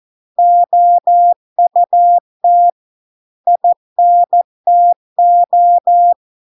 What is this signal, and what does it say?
Morse code, heard as a keyed tone, about 14 wpm, 700 Hz. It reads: OUT INTO